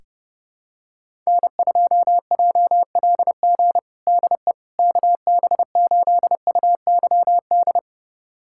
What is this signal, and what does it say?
Morse code, heard as a keyed tone, about 30 wpm, 700 Hz. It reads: N2JLG DE KB8UYD
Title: USA Callsign DE Callsign 30wpm